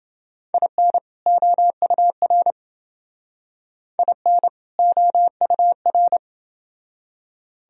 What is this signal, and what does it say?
Morse code, heard as a keyed tone, about 28 wpm, 700 Hz. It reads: IN OUR IN OUR